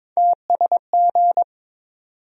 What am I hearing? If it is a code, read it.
Morse code, heard as a keyed tone, about 22 wpm, 700 Hz. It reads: TSG
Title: USA Call Sign Postfixes - 3 Letters 22wpm